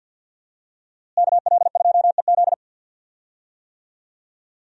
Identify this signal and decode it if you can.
Morse code, heard as a keyed tone, about 52 wpm, 700 Hz. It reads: KL2EZ
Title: USA Call Signs - 5 Characters - Repeated 3x - 50wpm